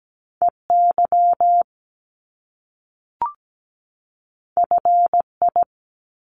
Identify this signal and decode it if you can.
Morse code, heard as a keyed tone, about 17 wpm, 700 Hz. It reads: EY FI
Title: Top Two Letter Combinations in Words 17wpm